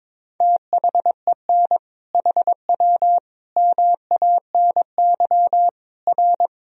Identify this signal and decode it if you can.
Morse code, heard as a keyed tone, about 22 wpm, 700 Hz. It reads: THEN HW MANY R
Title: Sentences from Top 100 Words and Abbreviations 22wpm